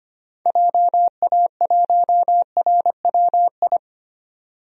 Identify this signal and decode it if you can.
Morse code, heard as a keyed tone, about 25 wpm, 700 Hz. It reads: JA1RWI
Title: DXCC-100 Call Signs - 6 Characters - 25wpm